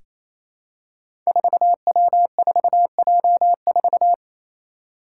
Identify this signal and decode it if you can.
Morse code, heard as a keyed tone, about 27 wpm, 700 Hz. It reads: 4W4J4
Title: Five Challenging Letter-Number Combinations 28wpm